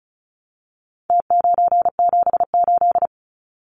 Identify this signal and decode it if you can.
Morse code, heard as a keyed tone, about 34 wpm, 700 Hz. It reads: T978
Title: Numbers - Contest Style - 4-Digit 35wpm